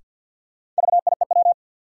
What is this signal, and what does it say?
Morse code, heard as a keyed tone, about 50 wpm, 700 Hz. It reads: VIEW
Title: Sentences from Top 900 Words - New Word Drill - 50wpm